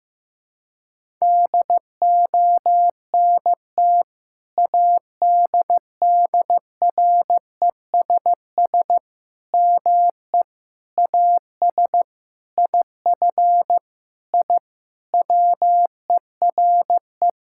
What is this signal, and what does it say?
Morse code, heard as a keyed tone, about 15 wpm, 700 Hz. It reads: DONT ADDRESS ME AS IF I WERE
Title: Sentences from Top 1000 Words - Review Part 6 - 15wpm